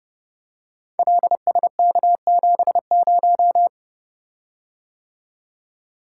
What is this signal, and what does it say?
Morse code, heard as a keyed tone, about 30 wpm, 700 Hz. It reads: LSK70
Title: Code Groups - 5 Letter:Numbers - 30wpm